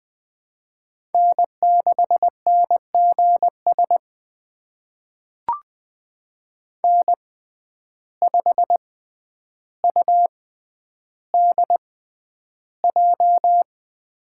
Morse code, heard as a keyed tone, about 21 wpm, spaced as for 10 wpm, 700 Hz. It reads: N6NGS N5UDJ
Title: USA Call Signs - 6x Character Spacing - 20wpm